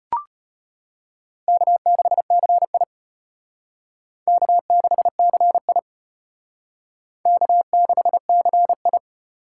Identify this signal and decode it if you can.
Morse code, heard as a keyed tone, about 31 wpm, 700 Hz. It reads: K6CI K6CI K6CI
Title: Speed Racing - USA 1x2 Call Signs 25wpm